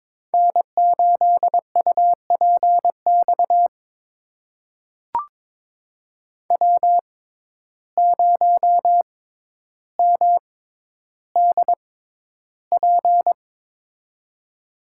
Morse code, heard as a keyed tone, about 22 wpm, spaced as for 11 wpm, 700 Hz. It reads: N8UPX W0MDP
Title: USA Call Signs - 5 Characters - 6x Character Spacing - 22wpm